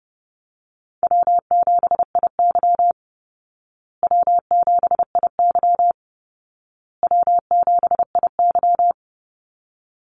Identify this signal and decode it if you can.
Morse code, heard as a keyed tone, about 30 wpm, 700 Hz. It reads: W7IY W7IY W7IY
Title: Easy Call Signs Repeated 3x - 30wpm